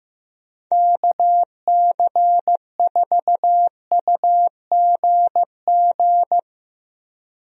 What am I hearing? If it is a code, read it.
Morse code, heard as a keyed tone, about 15 wpm, 700 Hz. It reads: KC4UGG
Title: USA Call Signs - 6 Characters - 15wpm